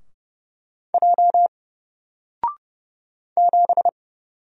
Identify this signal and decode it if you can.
Morse code, heard as a keyed tone, about 29 wpm, 700 Hz. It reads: J 7
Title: Single Letter-Number Alt 30wpm